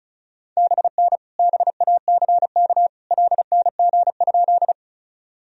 Morse code, heard as a keyed tone, about 34 wpm, 690 Hz. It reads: BN BACK LNG?
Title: Sentences from Top 200 Words and Abbreviations 35wpm